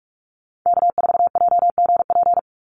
Morse code, heard as a keyed tone, about 44 wpm, 700 Hz. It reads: K4JGP
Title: USA Call Signs 45wpm